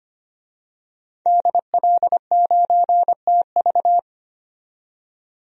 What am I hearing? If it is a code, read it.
Morse code, heard as a keyed tone, about 25 wpm, 700 Hz. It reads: DL9TV
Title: DXCC-100 Call Signs - 5 Characters - 25wpm